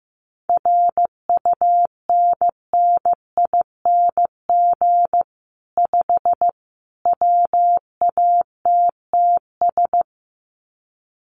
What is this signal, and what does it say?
Morse code, heard as a keyed tone, about 15 wpm, 700 Hz. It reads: RUNNING 5 WATTS
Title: Common QSO Phrases 15wpm